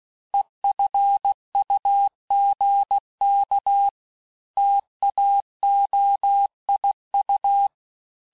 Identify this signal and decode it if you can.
Morse code, heard as a keyed tone, about 16 wpm, 800 Hz. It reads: EFUGK TAOIU
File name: cours-du-lundi files / cours-1592--teste-chiffre-vitesse-15mpm